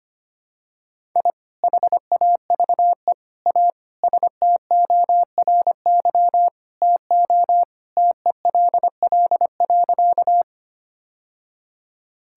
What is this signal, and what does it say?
Morse code, heard as a keyed tone, about 25 wpm, 700 Hz. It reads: I HAVE A STORY TO TELL.
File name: practice / 25wpm / Sentences with 2-Word Phrases - 25wpm